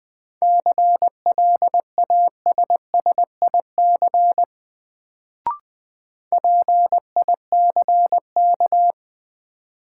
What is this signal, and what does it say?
Morse code, heard as a keyed tone, about 20 wpm, 700 Hz. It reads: CLASSIC PICK
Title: Sentences from Top 1000 Words - New Word Drill - 20wpm